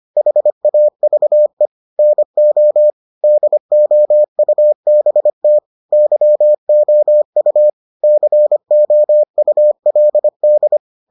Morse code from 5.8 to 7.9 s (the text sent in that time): YOU